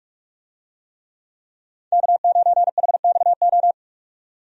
Morse code, heard as a keyed tone, about 46 wpm, 700 Hz. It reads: K9HXO
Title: USA Call Signs - 5 Characters - 45wpm